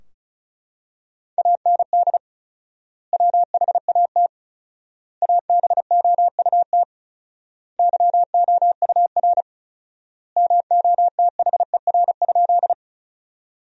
Morse code, heard as a keyed tone, about 36 wpm, 700 Hz. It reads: AND WHAT ABOUT YOUR MOTHER?
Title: 4x Word-Spacing - Sentences from Top 200 Words - 35wpm